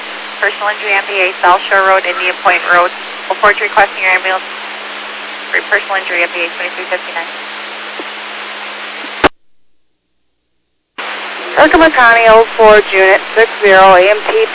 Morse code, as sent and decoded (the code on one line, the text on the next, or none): none